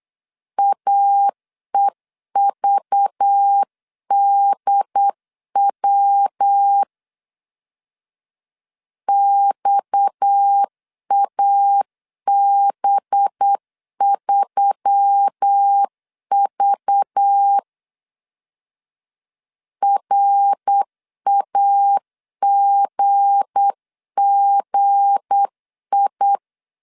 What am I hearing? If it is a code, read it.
Morse code, heard as a keyed tone, about 8 wpm, 800 Hz. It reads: AEVDW XAB3V RAGGI
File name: LEZIONI / 7 GF9Z / 7 G 8wpm 3118